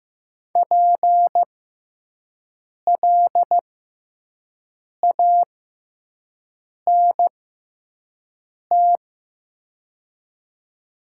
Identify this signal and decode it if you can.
Morse code, heard as a keyed tone, about 15 wpm, spaced as for 5 wpm, 700 Hz. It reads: PLANT
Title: 6x Character Spacing - Top 300 Words - 15wpm